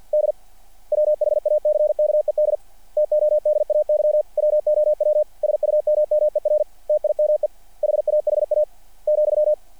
Morse code, heard as a keyed tone, about 50 wpm, 600 Hz. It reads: D WHACKED TODAY WOW SUMMER TIME HAHA ,
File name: archived / CRSnet